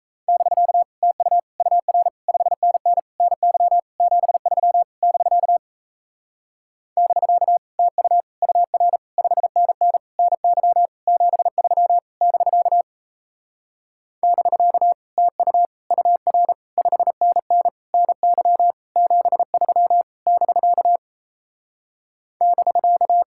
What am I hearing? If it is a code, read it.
Morse code, heard as a keyed tone, about 37 wpm, 700 Hz. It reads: <BK> TU UR 5NN NY 73 <BK> <BK> TU UR 5NN NY 73 <BK> <BK> TU UR 5NN NY 73 <BK> <BK>